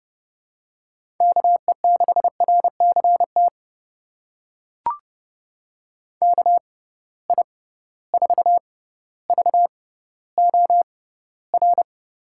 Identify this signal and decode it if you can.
Morse code, heard as a keyed tone, about 32 wpm, spaced as for 16 wpm, 700 Hz. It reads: KE6RCT KI4VOR
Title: USA Call Signs - 6 Characters - 6x Character Spacing - 30wpm